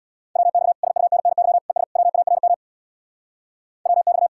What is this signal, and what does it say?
Morse code, heard as a keyed tone, about 79 wpm, 700 Hz. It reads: WX SUNNY ES WARM WX